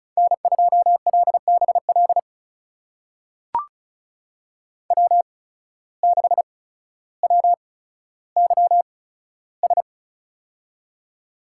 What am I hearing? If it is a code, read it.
Morse code, heard as a keyed tone, about 36 wpm, spaced as for 15 wpm, 700 Hz. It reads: N2LBL W6WYS
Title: USA Call Signs - 8x Character Spacing - 35wpm